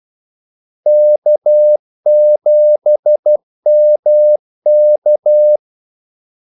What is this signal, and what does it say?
Morse code, heard as a keyed tone, about 12 wpm, 600 Hz. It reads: K7MK
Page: 2 ekstra ordmellemrum mellem kaldesignalerne